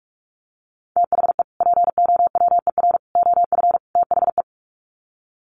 Morse code, heard as a keyed tone, about 44 wpm, 700 Hz. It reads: THE POWER OF THE
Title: Sets of 4 Words 45wpm